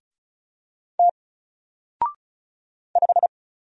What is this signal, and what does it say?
Morse code, heard as a keyed tone, about 34 wpm, 700 Hz. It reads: T 5
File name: Single-Letter-Number-35wpm